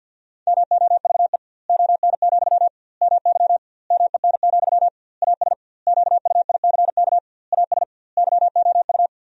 Morse code, heard as a keyed tone, about 52 wpm, 700 Hz. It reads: MOVE ON, MY MEN, AS QUICK AS YOU
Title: 2x Word-Spacing - Sentences from Top 500 Words - 50wpm